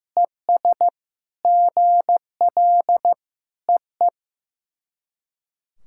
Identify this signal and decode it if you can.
Morse code, heard as a keyed tone, about 15 wpm, 700 Hz. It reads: ES GL EE
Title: SOTA QSO Practice 15wpm